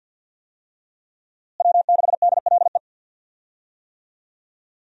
Morse code, heard as a keyed tone, about 52 wpm, 700 Hz. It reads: W6DLE
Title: USA Call Signs - 5 Characters - Repeated 3x - 50wpm